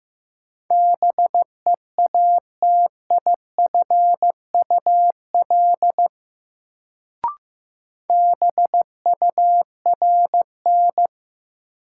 Morse code, heard as a keyed tone, about 15 wpm, 700 Hz. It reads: BEATIFUL BURN
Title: Sentences from Top 1300 Words - New Word Drill Alt - 15wpm